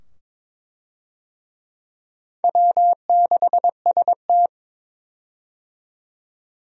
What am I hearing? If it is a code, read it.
Morse code, heard as a keyed tone, about 22 wpm, 700 Hz. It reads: W6ST
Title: USA Call Signs - 4 Characters - 22wpm